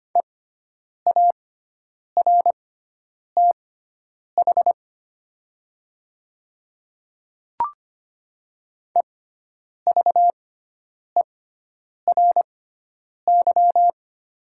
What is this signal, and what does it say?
Morse code, heard as a keyed tone, about 25 wpm, spaced as for 9 wpm, 690 Hz. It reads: EARTH EVERY
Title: 6x Character Spacing - Top 300 Words - 25wpm